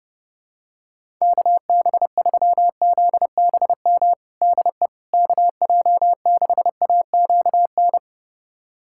Morse code, heard as a keyed tone, about 30 wpm, 700 Hz. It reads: KB3ZBM DE KJ6AQN
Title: USA Callsign DE Callsign - 6 Characters - 30wpm